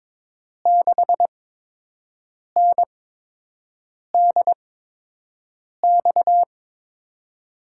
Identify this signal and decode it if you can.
Morse code, heard as a keyed tone, about 22 wpm, spaced as for 6 wpm, 700 Hz. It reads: 6NDX
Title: USA Call Signs - 6 Characters - 8x Character Spacing - 22wpm